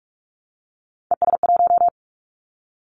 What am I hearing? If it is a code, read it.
Morse code, heard as a keyed tone, about 44 wpm, 700 Hz. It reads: ES1
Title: Callsign Prefixes for Top 100 DXCC - 45wpm